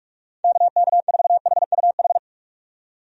Morse code, heard as a keyed tone, about 45 wpm, 700 Hz. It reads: KK4HUH
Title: USA Call Signs - 6 Characters - 45wpm